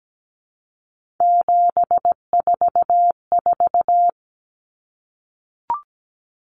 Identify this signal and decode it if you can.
Morse code, heard as a keyed tone, about 17 wpm, 700 Hz. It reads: 744
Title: Numbers - 3-Digit 17wpm